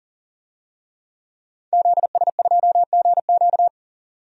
Morse code, heard as a keyed tone, about 40 wpm, 700 Hz. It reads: ZS2GQ
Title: DXCC-100 Call Signs - 5 Characters - 40wpm